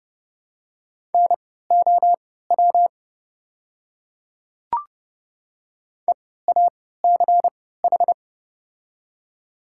Morse code, heard as a keyed tone, about 32 wpm, spaced as for 17 wpm, 700 Hz. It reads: NOW EACH
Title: 3x Character Spacing - Top 100 Words - 30wpm